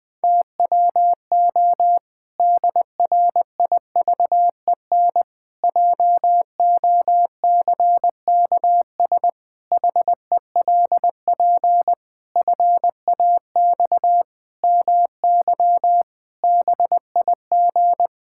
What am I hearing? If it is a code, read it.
Morse code, heard as a keyed tone, about 20 wpm, 700 Hz. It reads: TWO DRIVEN JOCKS HELP FAX MY BIG
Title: Pangram Challenge v2 - 20wpm